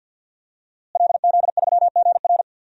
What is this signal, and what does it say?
Morse code, heard as a keyed tone, about 52 wpm, 700 Hz. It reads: LZ3GR
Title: DXCC-100 Call Signs - 5 Characters - Repeated 3x - 50wpm